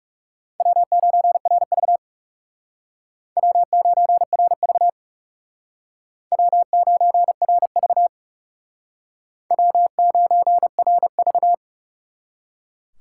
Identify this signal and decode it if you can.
Morse code, heard as a keyed tone, about 39 wpm, 700 Hz. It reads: W9RV W9RV W9RV W9RV